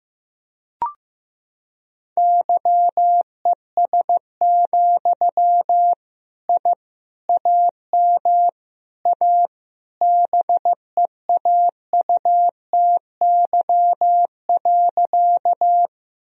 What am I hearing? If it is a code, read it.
Morse code, heard as a keyed tone, about 16 wpm, 700 Hz. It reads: YES, I AM A BEAUTY.
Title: Sentences from Top 1000 Words - Review Part 2 - 15wpm